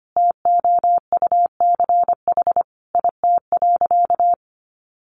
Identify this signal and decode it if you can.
Morse code, heard as a keyed tone, about 25 wpm, 700 Hz. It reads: TOUCH IT.
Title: Sentences from Top 1200 Words 25wpm